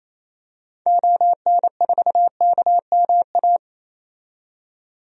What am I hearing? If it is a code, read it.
Morse code, heard as a keyed tone, about 28 wpm, 700 Hz. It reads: ON4KMA
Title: DXCC-100 Call Signs - 6 Characters - 28wpm